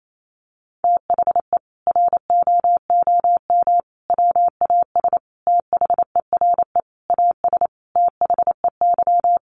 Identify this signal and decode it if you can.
Morse code, heard as a keyed tone, about 28 wpm, 700 Hz. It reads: THE ROOM WAS THERE AS THEY
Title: A Sound Of Thunder 28wpm